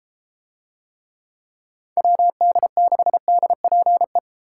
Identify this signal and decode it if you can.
Morse code, heard as a keyed tone, about 33 wpm, 700 Hz. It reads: WD6DPE